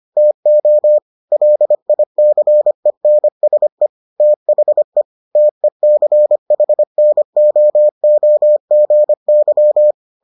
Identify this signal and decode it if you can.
Morse code, heard as a keyed tone, about 25 wpm, 600 Hz. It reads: TO LICENSE THE TECHNOOGY